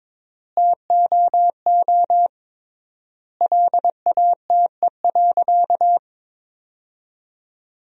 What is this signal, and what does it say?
Morse code, heard as a keyed tone, about 22 wpm, 700 Hz. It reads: TOO LATE.
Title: 3x Word-Spacing - Sentences from Top 300 Words - 22wpm